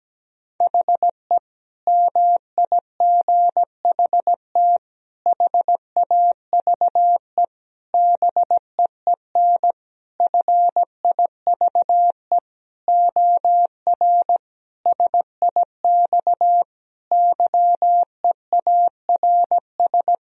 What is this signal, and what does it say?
Morse code, heard as a keyed tone, about 17 wpm, 700 Hz. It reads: HE MIGHT HAVE BEEN FIVE OR SIX YEARS
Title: Sentences from Top 1000 Words - Review Part 2 - 17wpm